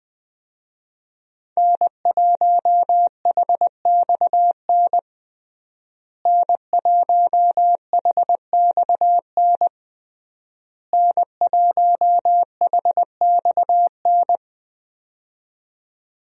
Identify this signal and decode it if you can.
Morse code, heard as a keyed tone, about 20 wpm, 700 Hz. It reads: N1HXN N1HXN N1HXN
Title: USA Call Signs - 5 Characters - Repeated 3x - 20wpm